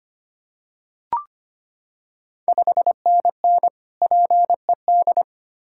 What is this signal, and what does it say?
Morse code, heard as a keyed tone, about 25 wpm, 700 Hz. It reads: 5NN PED
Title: Ontario QSO Party Exchange 25wpm